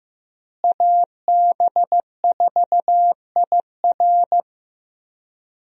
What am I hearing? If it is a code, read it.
Morse code, heard as a keyed tone, about 15 wpm, 700 Hz. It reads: AB4IR
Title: USA Call Signs - 5 Characters - Repeated 2x - 15wpm